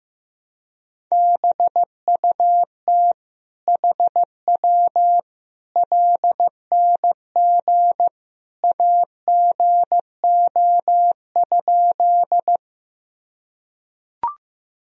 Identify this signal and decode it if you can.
Morse code, heard as a keyed tone, about 16 wpm, 700 Hz. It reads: BUT HW LNG AGO?
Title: Sentences from Top 500 Words and Abbreviations 15wpm